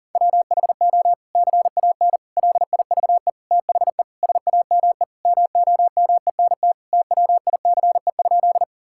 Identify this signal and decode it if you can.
Morse code, heard as a keyed tone, about 40 wpm, 700 Hz. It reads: WHO CAN LIVE THE SAME MOMENT TWICE?
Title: Sentences from Top 1600 Words 40wpm